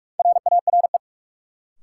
Morse code, heard as a keyed tone, about 47 wpm, 700 Hz. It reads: RARE
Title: Sentences from Top 1100 Words - New Word Drill - 45wpm